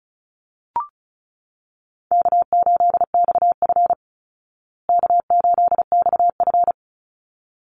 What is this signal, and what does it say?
Morse code, heard as a keyed tone, about 34 wpm, 700 Hz. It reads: K8XF K8XF
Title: Call Signs Repeated 3x - 35wpm